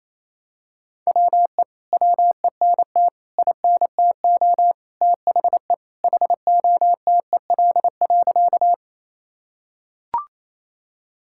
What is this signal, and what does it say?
Morse code, heard as a keyed tone, about 28 wpm, 700 Hz. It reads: WE WENT INTO THE HOTEL.
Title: Sentences from Top 1600 Words 28wpm